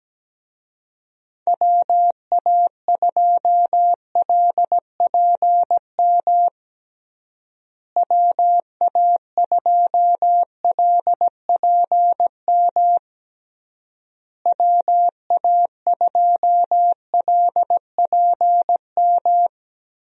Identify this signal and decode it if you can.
Morse code, heard as a keyed tone, about 17 wpm, 700 Hz. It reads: WA2LPM WA2LPM WA2LPM
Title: USA Call Signs - 6 Characters - Repeated 3x - 17wpm